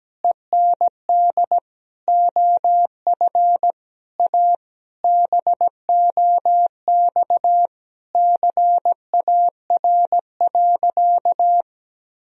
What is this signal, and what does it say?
Morse code, heard as a keyed tone, about 17 wpm, 700 Hz. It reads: END OF A BOX CAR.
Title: Sentences from Top 1000 Words - Review Part 2 - 17wpm